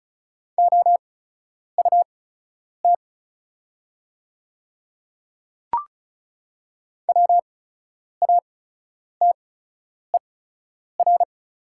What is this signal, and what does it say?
Morse code, heard as a keyed tone, about 36 wpm, spaced as for 10 wpm, 700 Hz. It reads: OUT WATER
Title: 8x Character Spacing - Top 100 Words - 35wpm